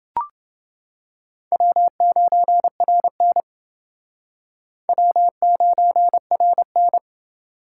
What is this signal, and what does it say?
Morse code, heard as a keyed tone, about 29 wpm, 700 Hz. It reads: W9RN W9RN